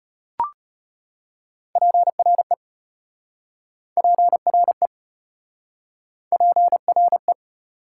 Morse code, heard as a keyed tone, about 33 wpm, 700 Hz. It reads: PRE PRE PRE